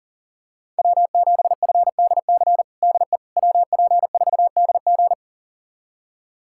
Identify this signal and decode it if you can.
Morse code, heard as a keyed tone, about 39 wpm, 700 Hz. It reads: W7FDC DE WP4DG
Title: USA Callsign DE Callsign - 5 Characters - 40wpm